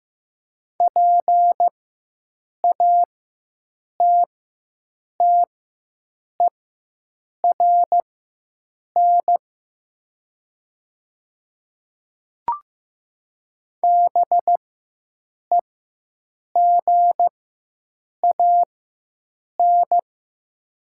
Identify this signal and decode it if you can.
Morse code, heard as a keyed tone, about 15 wpm, spaced as for 7 wpm, 700 Hz. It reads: PATTERN BEGAN
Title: 4x Character Spacing - Top 500 Words - 15wpm